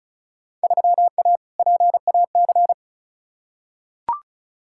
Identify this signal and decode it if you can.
Morse code, heard as a keyed tone, about 36 wpm, 700 Hz. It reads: 3A PAC
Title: Field Day Exchanges - 35wpm